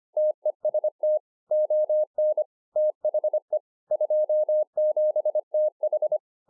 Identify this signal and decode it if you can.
Morse code, heard as a keyed tone, about 25 wpm, 610 Hz. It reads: TEST ON THE 27TH